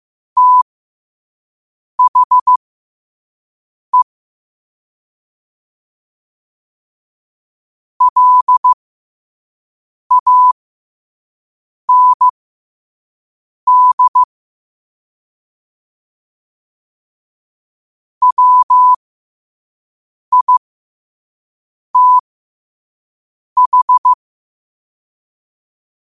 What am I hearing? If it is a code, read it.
Morse code, heard as a keyed tone, about 15 wpm, spaced as for 5 wpm, 1000 Hz. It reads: THE LAND WITH
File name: qotdfiles / 5wpm